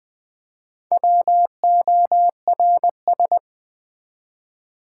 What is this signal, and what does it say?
Morse code, heard as a keyed tone, about 20 wpm, 700 Hz. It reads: WORS
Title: Build A Word - 1 Letter At A Time - 5-Letter Words - 20wpm